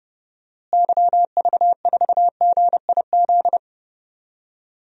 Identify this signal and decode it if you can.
Morse code, heard as a keyed tone, about 30 wpm, 700 Hz. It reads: YV4GIZ
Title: DXCC-100 Call Signs - 6 Characters - Repeated 2x - 30wpm